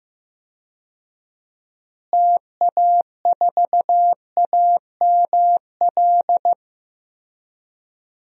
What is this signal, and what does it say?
Morse code, heard as a keyed tone, about 15 wpm, 700 Hz. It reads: TA4AML
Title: DXCC-100 Call Signs - 6 Characters - Repeated 3x - 15wpm